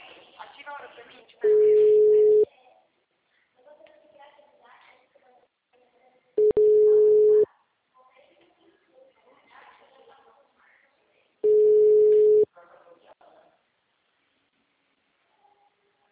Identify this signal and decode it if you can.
Ringback tone (Europe, 425 Hz), with speech.